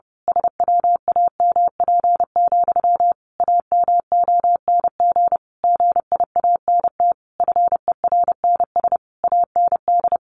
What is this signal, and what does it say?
Morse code, heard as a keyed tone, about 30 wpm, 700 Hz. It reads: SWAMP, AMONG GIANT FERNS AND
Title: A Sound Of Thunder 30wpm